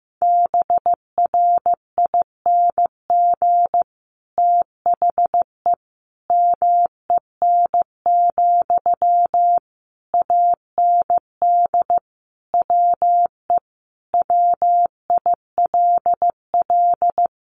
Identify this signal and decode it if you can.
Morse code, heard as a keyed tone, about 15 wpm, 700 Hz. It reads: BRING THE MEN, AND WE WILL
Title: Sentences from Top 1400 Words 15wpm